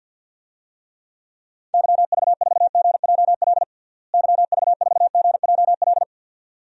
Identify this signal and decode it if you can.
Morse code, heard as a keyed tone, about 52 wpm, 700 Hz. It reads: YV4GJL YV4GJL
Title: DXCC-100 Call Signs - 6 Characters - Repeated 2x - 50wpm